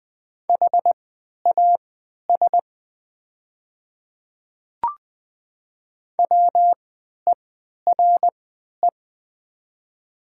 Morse code, heard as a keyed tone, about 20 wpm, spaced as for 11 wpm, 700 Hz. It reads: HAS WERE
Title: 3x Character Spacing - Top 100 Words - 20wpm